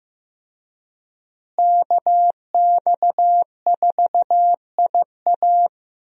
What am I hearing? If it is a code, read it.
Morse code, heard as a keyed tone, about 15 wpm, 700 Hz. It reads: KX4IA
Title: USA Call Signs - 5 Characters - Repeated 3x - 15wpm